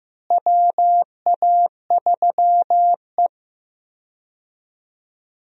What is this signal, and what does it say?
Morse code, heard as a keyed tone, about 15 wpm, 700 Hz. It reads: WA3E
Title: USA Call Signs - 4 Characters - Repeated 2x - 15wpm